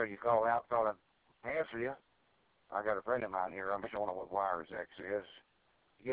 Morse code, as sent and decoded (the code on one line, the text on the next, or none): none